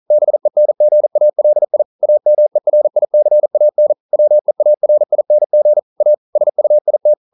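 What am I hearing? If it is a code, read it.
Morse code, heard as a keyed tone, about 42 wpm, 600 Hz. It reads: BENGALI AMERICAN WEARING A SUIT
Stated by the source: Morse code practice at 41 wpm, from CW med Gnister.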